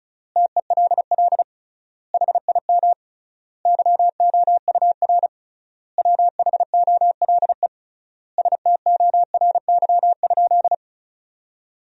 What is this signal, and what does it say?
Morse code, heard as a keyed tone, about 34 wpm, 700 Hz. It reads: TELL HIM YOUR WHOLE STORY?
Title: 3x Word-Spacing - Sentences from Top 400 Words - 35wpm